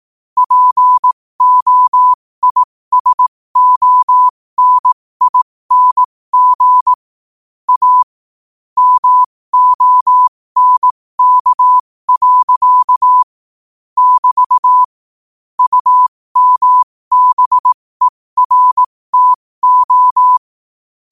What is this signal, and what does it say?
Morse code, heard as a keyed tone, about 18 wpm, spaced as for 15 wpm, 1000 Hz. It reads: POISONING A MONK. <BT> UMBERTO
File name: qotdfiles / 15wpm